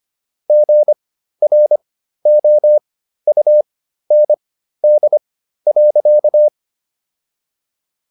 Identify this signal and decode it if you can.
Morse code, heard as a keyed tone, about 25 wpm, spaced as for 13 wpm, 600 Hz. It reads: GROUND.